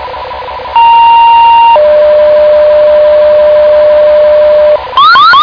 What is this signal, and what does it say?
Two-tone page, heard as a one-second tone, then a three-second tone, about 930 then 600 Hz, with noise. It is heard from 0.8 to 4.8 s.